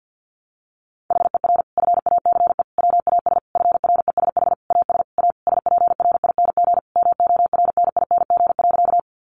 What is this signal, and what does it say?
Morse code, heard as a keyed tone, about 50 wpm, 700 Hz. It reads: HER FACE WAS FRESH AS A SPRING MORNING.
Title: Sentences from Top 1200 Words 50wpm